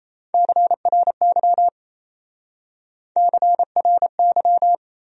Morse code, heard as a keyed tone, about 34 wpm, 700 Hz. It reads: CRY CRY
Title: Speed-Racing Top 500 Words 28wpm